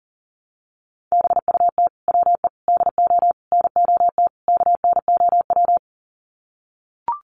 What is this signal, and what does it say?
Morse code, heard as a keyed tone, about 40 wpm, 700 Hz. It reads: BUT WE DO NOT KNOW
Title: Sets of 5 Words Encore2 40wpm